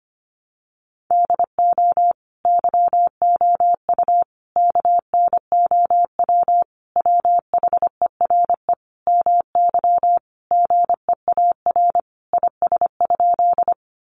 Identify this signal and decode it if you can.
Morse code, heard as a keyed tone, about 25 wpm, 700 Hz. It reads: DO YOU KNOW WHERE MY GEAR IS?
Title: Sentences from Top 1400 Words 25wpm